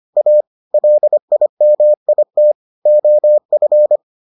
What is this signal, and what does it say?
Morse code, heard as a keyed tone, about 25 wpm, 600 Hz. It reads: A LIMIT OF